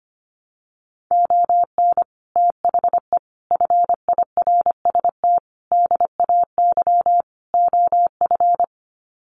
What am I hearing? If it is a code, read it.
Morse code, heard as a keyed tone, about 25 wpm, 700 Hz. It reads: ON THE FIRST DAY OF
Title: Sets of 5 Words Encore 25wpm